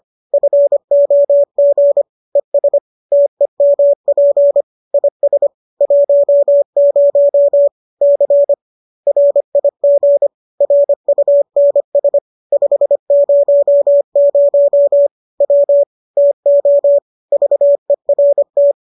Morse code, heard as a keyed tone, about 25 wpm, 570 Hz. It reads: FOG ES TEMP IS 10 C RIG RUNS 500 W TO VERT